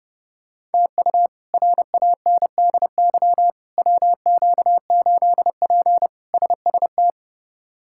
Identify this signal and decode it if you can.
Morse code, heard as a keyed tone, about 32 wpm, 700 Hz. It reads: TU RANDY WQ8P SST
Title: K1USN SST QSO Practice - 30wpm